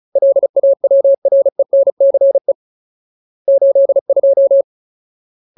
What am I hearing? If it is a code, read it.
Morse code, heard as a keyed tone, about 34 wpm, 550 Hz. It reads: LAWRENCE 82
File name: Ultimate Word List / 35WPM Ultimate